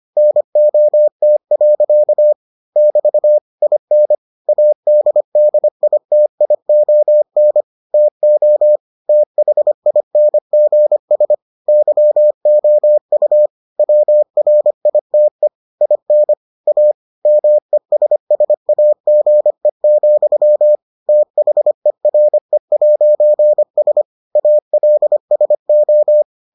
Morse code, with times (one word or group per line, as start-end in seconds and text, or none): 0.2-2.3 NOT.
2.8-3.4 <BT>
3.6-4.1 IN
4.5-7.6 ADDITION
7.9-8.8 TO
9.1-11.3 THINGS
11.7-13.5 YOU
13.8-15.5 WRITE
15.8-16.3 IN
16.7-16.9 A
17.3-20.8 MESSAGE,
21.1-24.0 THERE'S
24.4-26.2 ALSO